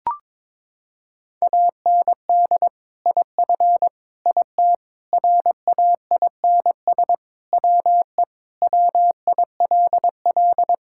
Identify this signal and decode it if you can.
Morse code, heard as a keyed tone, about 22 wpm, 700 Hz. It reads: AND IF IT RAINS WE WILL
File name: practice / 22wpm / Sentences with 2-Word Phrases - 22wpm